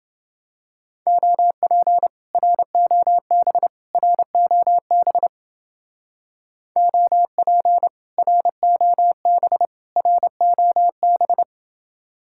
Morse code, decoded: OP ROB ROB OP ROB ROB

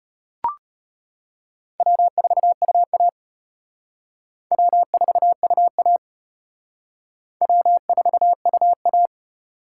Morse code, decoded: W4UA W4UA W4UA

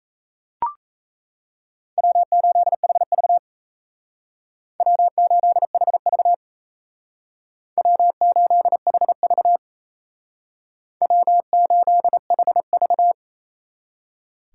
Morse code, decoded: W8HV W8HV W8HV W8HV

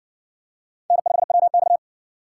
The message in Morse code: N5WX

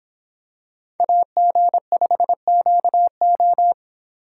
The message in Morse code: AG5QO

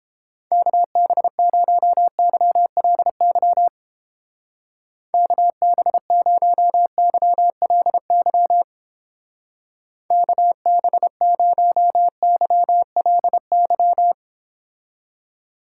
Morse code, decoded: KB0YLY KB0YLY KB0YLY

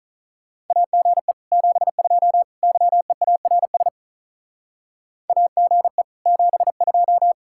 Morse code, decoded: AGE 72 YEARS AGE 72

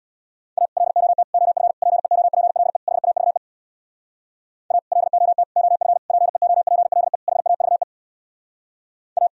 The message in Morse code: A LOT OF PEOPLE HAVE A LOT OF PEOPLE HAVE A

